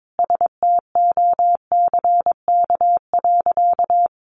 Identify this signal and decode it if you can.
Morse code, heard as a keyed tone, about 22 wpm, 700 Hz. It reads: STOCK.
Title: Sentences from Top 1300 Words 22wpm